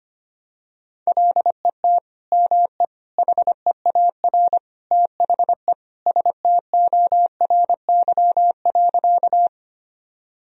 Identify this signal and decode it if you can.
Morse code, heard as a keyed tone, about 25 wpm, 700 Hz. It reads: LET ME HEAR THE STORY.
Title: Sentences from Top 1000 Words - Review Part 1 - 25wpm